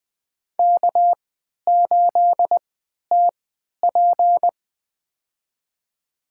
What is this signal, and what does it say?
Morse code, heard as a keyed tone, about 20 wpm, spaced as for 11 wpm, 700 Hz. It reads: K8TP
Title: USA Call Signs - 4 Characters - 3x Character Spacing - 20wpm